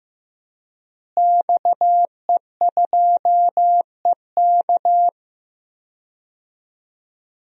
Morse code, heard as a keyed tone, about 15 wpm, 700 Hz. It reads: XE2EK
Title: DXCC-100 Call Signs - 5 Characters - Repeated 3x - 15wpm